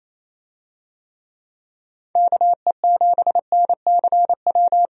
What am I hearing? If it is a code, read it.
Morse code, heard as a keyed tone, about 28 wpm, 700 Hz. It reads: KE7NCW
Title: USA Call Signs - 6 Characters - 28wpm